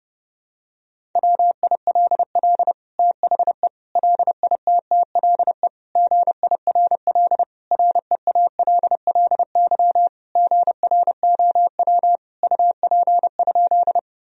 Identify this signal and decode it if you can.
Morse code, heard as a keyed tone, about 32 wpm, 700 Hz. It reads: WILL THE LITTLE GIRL REALLY GROW UP?
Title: Sentences from Top 1000 Words - Review Part 1 - 30wpm